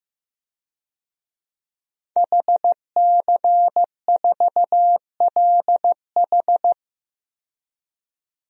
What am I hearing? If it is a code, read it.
Morse code, heard as a keyed tone, about 15 wpm, 700 Hz. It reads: HC4LH
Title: DXCC-100 Call Signs - 5 Characters - 15wpm